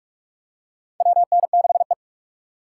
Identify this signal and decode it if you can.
Morse code, heard as a keyed tone, about 45 wpm, 700 Hz. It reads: WN6E
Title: USA Call Signs - 4 Characters - 3x Character Spacing - 45wpm